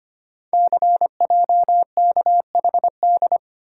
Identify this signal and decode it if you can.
Morse code, heard as a keyed tone, about 25 wpm, 700 Hz. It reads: CJKHD
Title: Code Groups - 5 Letter:Numbers - 25wpm